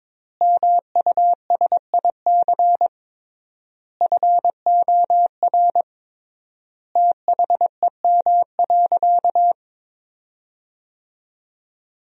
Morse code, heard as a keyed tone, about 22 wpm, 700 Hz. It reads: MUSIC FOR THEM.
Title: 3x Word-Spacing - Sentences from Top 300 Words - 22wpm